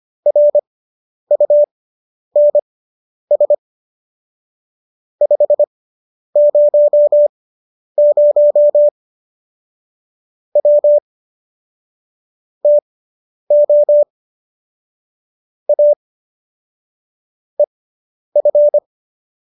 Morse code, heard as a keyed tone, about 25 wpm, spaced as for 10 wpm, 590 Hz. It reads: RUNS 500 W TO A EF